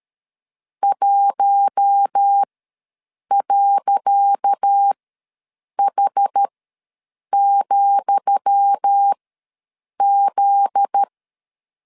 Morse code, heard as a keyed tone, about 13 wpm, spaced as for 7 wpm, 800 Hz. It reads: 1.H,Z
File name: LEZIONI / 10 SIGNS / 10D COMA ws 12wpm 3618